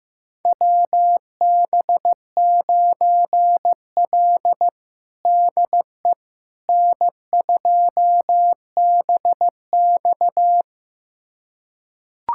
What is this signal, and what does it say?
Morse code, heard as a keyed tone, about 15 wpm, 700 Hz. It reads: WB9L DE N2BX
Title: USA Callsign DE Callsign - 4 Characters - 15wpm